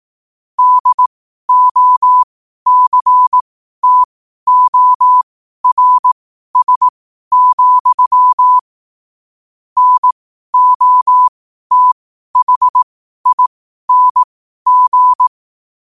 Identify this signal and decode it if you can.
Morse code, heard as a keyed tone, about 18 wpm, spaced as for 12 wpm, 1000 Hz. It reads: DOCTORS, NOTHING
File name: qotdfiles / 12wpm